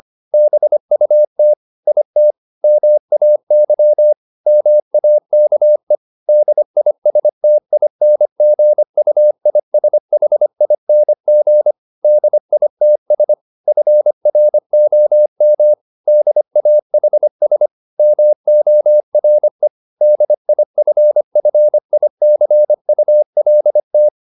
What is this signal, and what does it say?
Morse code, heard as a keyed tone, about 25 wpm, 600 Hz. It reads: BUT IT MAY MAKE DISTINGUISHING DITS FROM DAHS MORE DIFFICULT